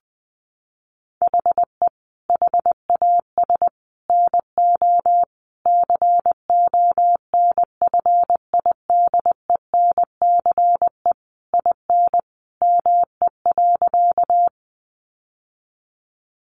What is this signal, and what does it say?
Morse code, heard as a keyed tone, about 20 wpm, 700 Hz. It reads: HE HAS NO CONFIDENCE IN ME.
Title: Sentences from Top 1500 Words 20wpm